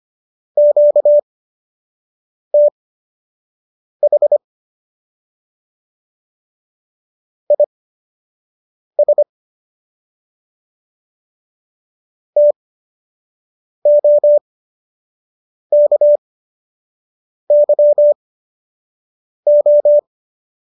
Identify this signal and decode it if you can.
Morse code, heard as a keyed tone, about 25 wpm, spaced as for 6 wpm, 590 Hz. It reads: QTH IS TOKYO
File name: qso003 06